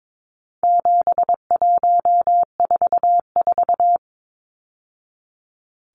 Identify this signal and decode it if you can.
Morse code, heard as a keyed tone, about 22 wpm, 700 Hz. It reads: 7144
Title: Numbers - 4-Digit 22wpm